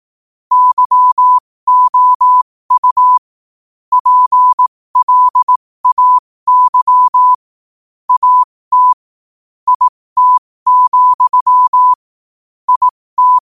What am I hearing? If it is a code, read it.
Morse code, heard as a keyed tone, about 18 wpm, spaced as for 15 wpm, 1000 Hz. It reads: YOU PLAY AT IT, IT
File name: qotdfiles / 15wpm